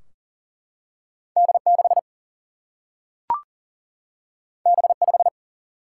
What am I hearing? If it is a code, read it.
Morse code, heard as a keyed tone, about 40 wpm, 700 Hz. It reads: D6 B5
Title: Single-Letter Single-Number 40wpm